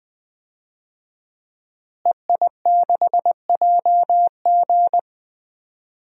Morse code, heard as a keyed tone, about 20 wpm, 700 Hz. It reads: EI6JG
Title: DXCC-100 Call Signs - 5 Characters - Repeated 3x - 20wpm